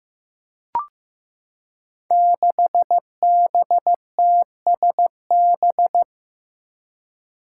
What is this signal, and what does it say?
Morse code, heard as a keyed tone, about 15 wpm, 700 Hz. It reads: 6BTSB
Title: Five Challenging Letter-Number Combinations 15wpm